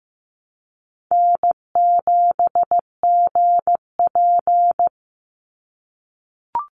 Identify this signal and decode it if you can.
Morse code, heard as a keyed tone, about 15 wpm, 700 Hz. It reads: N7GP
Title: USA Call Signs 15wpm